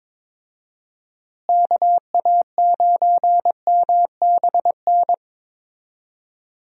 Morse code, heard as a keyed tone, about 22 wpm, 700 Hz. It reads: KA9MBN
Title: USA Call Signs - 6 Characters - Repeated 2x - 22wpm